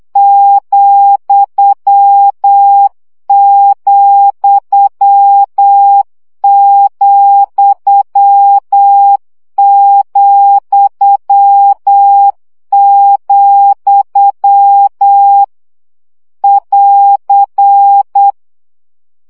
Morse code, heard as a keyed tone, about 8 wpm, 800 Hz. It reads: ,,,,, <AR>
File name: CW gyakorlas vesszo 7 baud